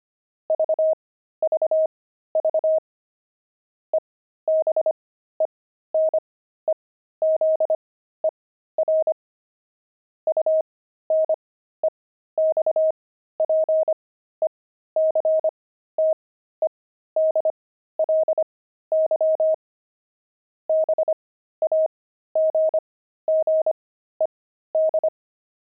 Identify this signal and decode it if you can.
Morse code, heard as a keyed tone, about 25 wpm, spaced as for 13 wpm, 630 Hz. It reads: VVV EBENEZER UNEXPECTEDLY BAGGED